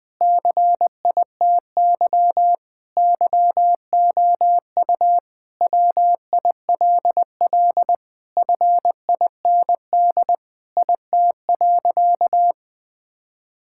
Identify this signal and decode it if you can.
Morse code, heard as a keyed tone, about 20 wpm, 700 Hz. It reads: CITY YOU WILL FIND IT.
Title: Sentences with 4-Word Phrases - 20wpm